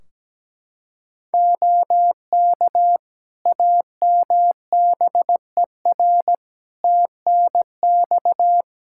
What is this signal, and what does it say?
Morse code, heard as a keyed tone, about 17 wpm, 700 Hz. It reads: OK AMBER TNX
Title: Practice QSO - Exchange 4 - Ending - 17wpm